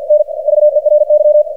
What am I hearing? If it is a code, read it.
Morse code, heard as a keyed tone, about 47 wpm, 600 Hz, with radio noise. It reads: N VERY
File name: archived / CRSnet